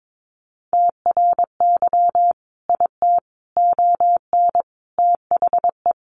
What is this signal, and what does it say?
Morse code, heard as a keyed tone, about 22 wpm, 700 Hz. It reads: TRY IT ON THE
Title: Sentences from Top 400 Words 22wpm